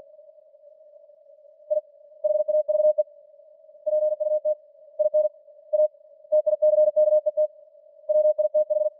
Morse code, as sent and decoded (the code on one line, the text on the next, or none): .. / .... .- ...- . / .--. ..- - / .. -. / .- / - .. -.-. -.- . - / .-- .. - ....
I HAVE PUT IN A TICKET WITH